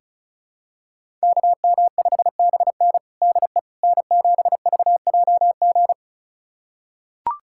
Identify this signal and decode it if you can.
Morse code, heard as a keyed tone, about 36 wpm, 700 Hz. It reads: KM5BN DE N7VJG
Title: USA Callsign DE Callsign - 5 Characters - 35wpm